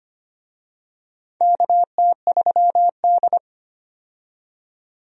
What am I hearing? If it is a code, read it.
Morse code, heard as a keyed tone, about 25 wpm, 700 Hz. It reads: KT3D
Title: USA Call Signs - 4 Characters - 25wpm